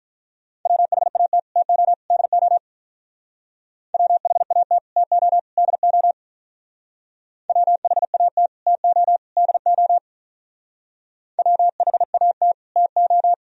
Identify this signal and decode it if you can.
Morse code, heard as a keyed tone, about 45 wpm, 700 Hz. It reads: WHAT TO DO WHAT TO DO WHAT TO DO WHAT TO